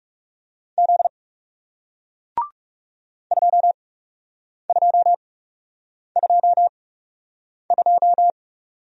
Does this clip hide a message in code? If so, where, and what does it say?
Yes: Morse code from 0.8 to 8.3 s, heard as a keyed tone, about 40 wpm, spaced as for 8 wpm, 700 Hz. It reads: Z 2222